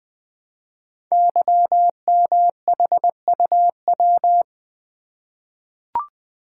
Morse code, heard as a keyed tone, about 20 wpm, 700 Hz. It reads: YMHUW
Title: Code Groups - 5 Letter:Numbers - 20wpm